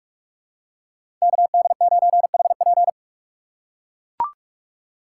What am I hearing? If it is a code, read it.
Morse code, heard as a keyed tone, about 43 wpm, 700 Hz. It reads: KD9HP
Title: USA Call Signs - 5 Characters - 2x Character Spacing - 45wpm